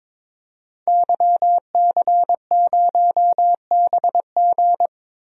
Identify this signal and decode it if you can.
Morse code, heard as a keyed tone, about 22 wpm, 700 Hz. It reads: YC0BG
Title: DXCC-100 Call Signs - 5 Characters - Repeated 2x - 22wpm